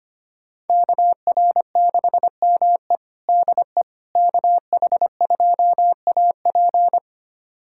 Morse code, heard as a keyed tone, about 25 wpm, 700 Hz. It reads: KR6ME DE KH2AP
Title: USA Callsign DE Callsign - 5 Characters - 25wpm